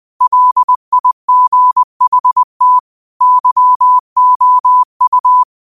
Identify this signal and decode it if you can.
Morse code, heard as a keyed tone, about 20 wpm, 1000 Hz. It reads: LIGHT YOU
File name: qotdfiles / 20wpm